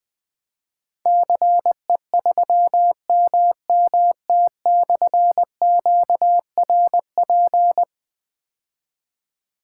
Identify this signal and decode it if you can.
Morse code, heard as a keyed tone, about 20 wpm, 700 Hz. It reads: CE3MMT/QRP
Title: Difficult Call Signs 20wpm